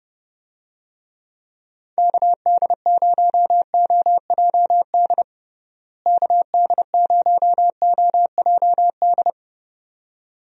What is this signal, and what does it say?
Morse code, heard as a keyed tone, about 29 wpm, 700 Hz. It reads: KD0OJD KD0OJD
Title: USA Call Signs - 6 Characters - Repeated 2x - 30wpm